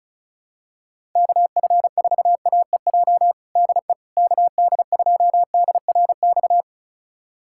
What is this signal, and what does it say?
Morse code, heard as a keyed tone, about 36 wpm, 700 Hz. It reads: KF4AEJ DE KD2DRX
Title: USA Callsign DE Callsign - 6 Characters - 35wpm